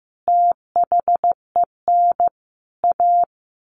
Morse code, heard as a keyed tone, about 15 wpm, 700 Hz. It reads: THEN A
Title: A Sound Of Thunder 15wpm